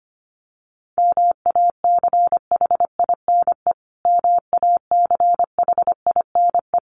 Morse code, heard as a keyed tone, about 25 wpm, 700 Hz. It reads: MACHINE MACHINE
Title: Top 500 Words Repeated 25wpm